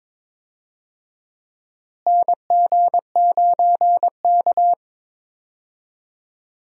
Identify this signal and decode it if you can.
Morse code, heard as a keyed tone, about 21 wpm, 700 Hz. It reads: NG9K
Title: USA Call Signs - 4 Characters - Repeated 2x - 22wpm